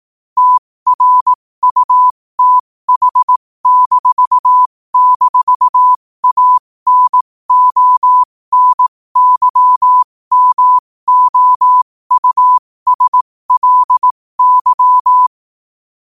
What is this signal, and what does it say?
Morse code, heard as a keyed tone, about 18 wpm, spaced as for 15 wpm, 1000 Hz. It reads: TRUTH--ANONYMOUSLY
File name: qotdfiles / 15wpm